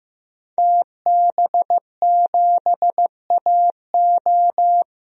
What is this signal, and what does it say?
Morse code, heard as a keyed tone, about 15 wpm, 700 Hz. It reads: TB7AO
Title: DXCC-100 Call Signs - 5 Characters - 15wpm